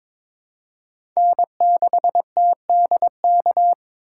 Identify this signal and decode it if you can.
Morse code, heard as a keyed tone, about 23 wpm, 700 Hz. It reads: N6TDK
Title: USA Call Signs - 5 Characters - 3x Character Spacing - 22wpm